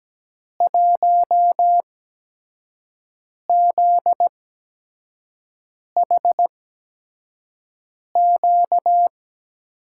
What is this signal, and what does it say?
Morse code, heard as a keyed tone, about 17 wpm, spaced as for 5 wpm, 700 Hz. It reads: 1ZHQ
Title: USA Call Signs - 6 Characters - 8x Character Spacing - 17wpm